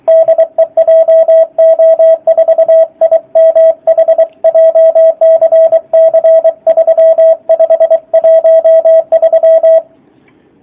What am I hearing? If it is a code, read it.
Morse code, heard as a keyed tone, about 24 wpm, 650 Hz, with radio noise. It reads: DE JO4IMH JCC 3513
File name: jh8cbh / 2019cw / sp27- w